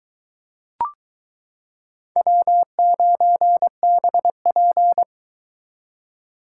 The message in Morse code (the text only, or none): W9BP